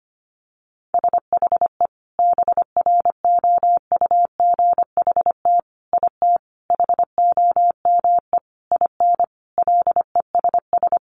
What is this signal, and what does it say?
Morse code, heard as a keyed tone, about 25 wpm, 700 Hz. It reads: SHE BROUGHT IT HOME IN LESS
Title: Sentences from Top 500 Words 25wpm